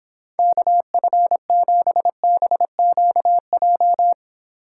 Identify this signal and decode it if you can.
Morse code, heard as a keyed tone, about 29 wpm, 700 Hz. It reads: KF7BQJ